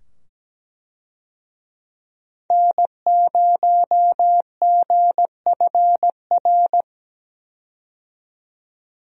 Morse code, heard as a keyed tone, about 17 wpm, 700 Hz. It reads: N0GFR
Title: USA Call Signs - 5 Characters - 17wpm